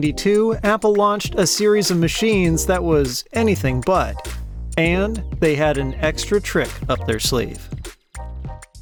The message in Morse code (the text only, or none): none